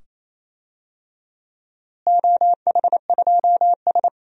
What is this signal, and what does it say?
Morse code, heard as a keyed tone, about 28 wpm, 700 Hz. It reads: OH2S
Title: DXCC-100 Call Signs - 4 Characters - Repeated 2x - 28wpm